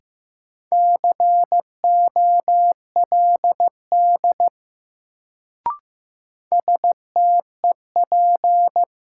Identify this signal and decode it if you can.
Morse code, heard as a keyed tone, about 15 wpm, 700 Hz. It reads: COLD STEP
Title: Sentences from Top 400 Words - New Word Drill - 15wpm